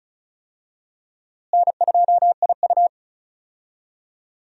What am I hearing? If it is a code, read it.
Morse code, heard as a keyed tone, about 36 wpm, 700 Hz. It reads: N2IU
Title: USA Call Signs - 4 Characters - Repeated 2x - 35wpm